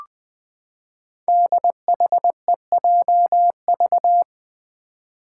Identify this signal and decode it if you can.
Morse code, heard as a keyed tone, about 20 wpm, 700 Hz. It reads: DHEJV
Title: Five Challenging Letter-Number Combinations 20wpm